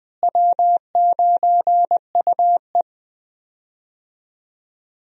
Morse code, heard as a keyed tone, about 20 wpm, 700 Hz. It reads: W9UE
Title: USA Call Signs - 4 Characters - Repeated 3x - 20wpm